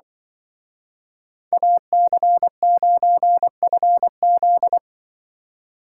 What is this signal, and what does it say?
Morse code, heard as a keyed tone, about 23 wpm, 700 Hz. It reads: AC9FZ
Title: Speed Racing - USA Call Signs 20wpm